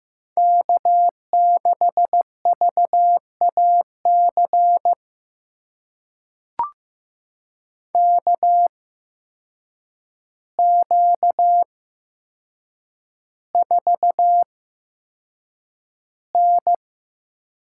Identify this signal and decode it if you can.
Morse code, heard as a keyed tone, about 15 wpm, spaced as for 7 wpm, 700 Hz. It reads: K6VAC KQ4N
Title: USA Call Signs - 8x Character Spacing - 15wpm